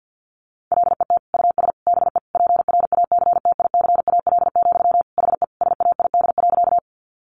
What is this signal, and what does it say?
Morse code, heard as a keyed tone, about 50 wpm, 700 Hz. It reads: LET US BE PRACTICAL, HE SAID.
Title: Sentences from Top 1400 Words 50wpm